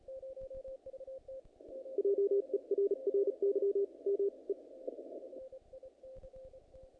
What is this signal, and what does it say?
Morse code, heard as a keyed tone, about 36 wpm, 400 Hz. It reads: JERRY M E E
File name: contests / naqp-cw-k5tr-op-k5pi- -2000